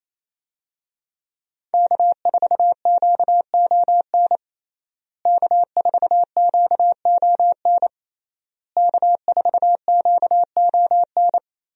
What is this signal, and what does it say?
Morse code, heard as a keyed tone, about 28 wpm, 700 Hz. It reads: K4QON K4QON K4QON
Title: USA Call Signs - 5 Characters - Repeated 3x - 28wpm